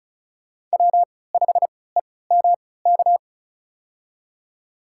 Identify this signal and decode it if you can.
Morse code, heard as a keyed tone, about 34 wpm, spaced as for 20 wpm, 700 Hz. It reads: W5EMK
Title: USA Call Signs - 5 Characters - 3x Character Spacing - 35wpm